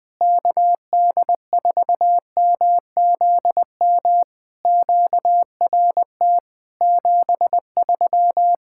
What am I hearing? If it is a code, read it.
Morse code, heard as a keyed tone, about 20 wpm, 700 Hz. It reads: KD4MZM QRT 73
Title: POTA QSO Practice - 20wpm